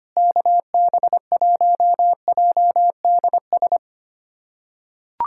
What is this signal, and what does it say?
Morse code, heard as a keyed tone, about 25 wpm, 700 Hz. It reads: KB1JDS
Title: USA Callsign DE Callsign 25wpm